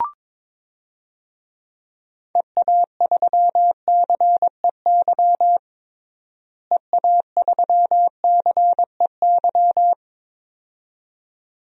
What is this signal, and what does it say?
Morse code, heard as a keyed tone, about 21 wpm, 700 Hz. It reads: EA3CEY EA3CEY
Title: DXCC-100 Call Signs - 6 Characters - Repeated 2x - 22wpm